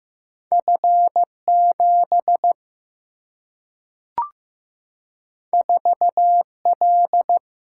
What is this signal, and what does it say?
Morse code, heard as a keyed tone, about 15 wpm, 700 Hz. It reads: F7 4L
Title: Single-Letter Single-Number 15wpm